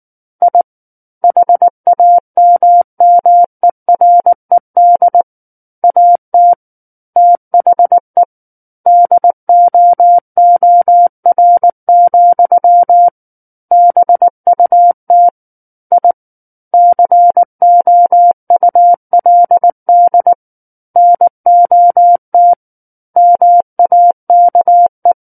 Morse code, heard as a keyed tone, about 19 wpm, 700 Hz. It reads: I HAMMERED AT THE DOOR, BUT I COULD NOT MAKE
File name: Book One / 10-Chapter 10 - 19 WPM